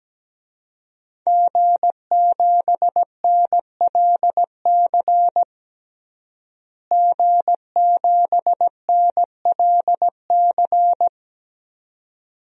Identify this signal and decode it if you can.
Morse code, heard as a keyed tone, about 17 wpm, 700 Hz. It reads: G7NLC G7NLC
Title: DXCC-100 Call Signs - 5 Characters - Repeated 2x - 17wpm